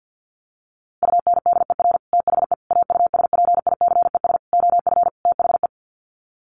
Morse code, heard as a keyed tone, about 50 wpm, 700 Hz. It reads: UNDER THE AUSPICES OF THE
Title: Sets of 5 Words Encore2 50wpm